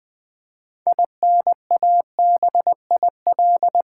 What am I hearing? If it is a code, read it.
Morse code, heard as a keyed tone, about 20 wpm, 700 Hz. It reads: INABIL
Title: Build A Word - 3 Letters At A Time - 9-Letter Words - 20wpm